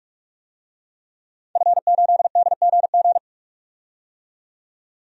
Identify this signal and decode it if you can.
Morse code, heard as a keyed tone, about 47 wpm, 700 Hz. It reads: F8DGG
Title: DXCC-100 Call Signs - 5 Characters - 45wpm